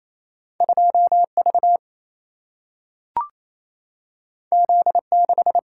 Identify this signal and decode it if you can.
Morse code, heard as a keyed tone, about 28 wpm, 700 Hz. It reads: 2 V Z 6
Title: Single-Letter Single-Number 28wpm